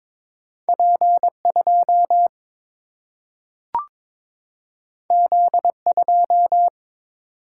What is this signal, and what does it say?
Morse code, heard as a keyed tone, about 22 wpm, 700 Hz. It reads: P 2 Z 2
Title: Single-Letter Single-Number 22wpm